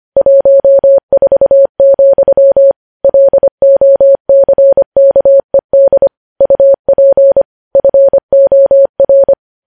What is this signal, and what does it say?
Morse code, heard as a keyed tone, about 25 wpm, 550 Hz. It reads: 14, LOCKED UP FOR
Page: news headlines in morse code at 25 wpm